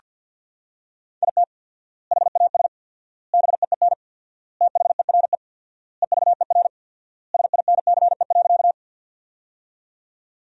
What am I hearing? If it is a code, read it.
Morse code, heard as a keyed tone, about 47 wpm, 700 Hz. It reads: IT HAS BEEN THERE EVER SINCE.
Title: 4x Word-Spacing - Sentences from Top 400 Words - 50wpm